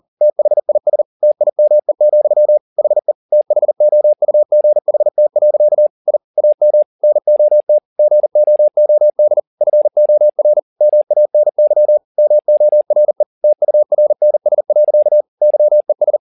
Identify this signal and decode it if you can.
Morse code, heard as a keyed tone, about 40 wpm, 600 Hz. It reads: THIS TIME, HE THOUGHT. I AM NOT GOOD FOR MANY MORE TURNS. YES
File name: the old man and the sea 2 40wpm